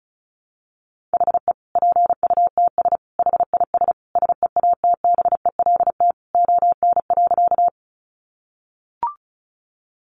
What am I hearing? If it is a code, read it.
Morse code, heard as a keyed tone, about 34 wpm, 700 Hz. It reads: HE PUTS HIS SEATBELT ON.
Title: Additional Easy Phrases 35wpm